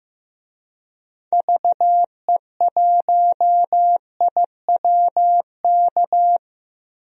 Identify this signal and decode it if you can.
Morse code, heard as a keyed tone, about 16 wpm, 700 Hz. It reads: VE1IWK
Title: DXCC-100 Call Signs - 6 Characters - Repeated 2x - 15wpm